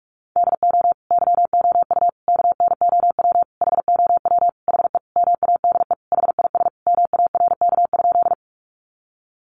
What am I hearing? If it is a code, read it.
Morse code, heard as a keyed tone, about 44 wpm, 700 Hz. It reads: DO YOU KNOW HOW HE MADE HIS MARK?
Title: Sentences from Top 300 Words 45wpm